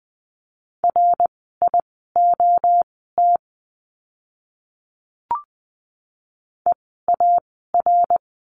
Morse code, heard as a keyed tone, about 20 wpm, spaced as for 14 wpm, 700 Hz. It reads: RIOT EAR
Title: 061 Lesson 11f - Letter h - Words 20wpm